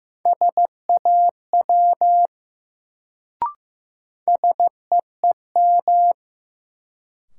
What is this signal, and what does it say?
Morse code, heard as a keyed tone, about 15 wpm, 700 Hz. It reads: SAW SEEM
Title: Sentences from Top 300 Words - New Word Drill - 15wpm